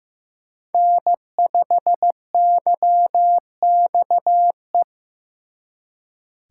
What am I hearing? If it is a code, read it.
Morse code, heard as a keyed tone, about 18 wpm, 700 Hz. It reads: N5YXE
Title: Speed Racing - USA 1x3 Call Signs 15wpm